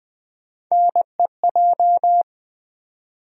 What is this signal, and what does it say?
Morse code, heard as a keyed tone, about 20 wpm, 700 Hz. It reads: NEJ
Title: USA Call Sign Postfixes - 3 Letters 20wpm